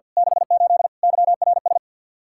Morse code, heard as a keyed tone, about 53 wpm, 700 Hz. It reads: 68 YRS
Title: Practice QSO - Exchange 3 - More Info - 50wpm